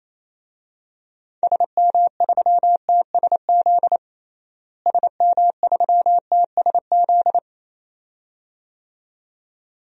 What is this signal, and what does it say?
Morse code, heard as a keyed tone, about 27 wpm, 700 Hz. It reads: SM3TSZ SM3TSZ
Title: DXCC-100 Call Signs - 6 Characters - Repeated 2x - 28wpm